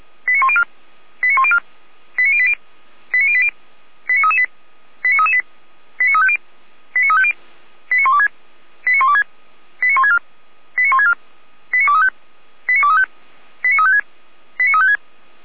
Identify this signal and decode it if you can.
Five-tone selective call (ZVEI), heard as quick runs of five tones, tones of about 70 ms. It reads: 89104 89104 89080 89080 89338 89338 89350 89350 89126 89126 89164 89164 89235 89235 89356 89356
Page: /455/SDR/ZVEI/89115/